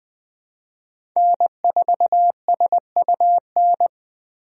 Morse code, heard as a keyed tone, about 20 wpm, 700 Hz. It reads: N4SUN
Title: USA Call Signs - 5 Characters - Repeated 3x - 20wpm